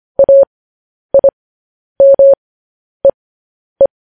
Morse code, heard as a keyed tone, about 25 wpm, spaced as for 10 wpm, 550 Hz. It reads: AIMEE
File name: podcast / News headlines-1x-25x10